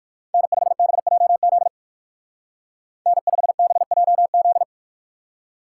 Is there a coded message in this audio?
Yes: Morse code from 0.3 to 4.6 s, heard as a keyed tone, about 62 wpm, 700 Hz. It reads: N5BJZ N5BJZ